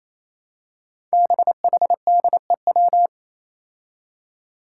Morse code, heard as a keyed tone, about 28 wpm, 700 Hz. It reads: BHDEW
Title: Five Challenging Letter-Number Combinations 28wpm